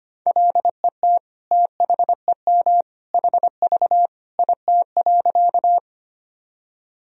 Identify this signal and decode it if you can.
Morse code, heard as a keyed tone, about 25 wpm, 700 Hz. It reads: LET THEM HV IT.
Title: Sentences from Top 300 Words and Abbreviations 25wpm